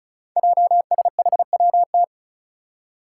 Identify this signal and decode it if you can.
Morse code, heard as a keyed tone, about 36 wpm, 700 Hz. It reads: JSHWT
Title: Five Challenging Letter-Number Combinations 35wpm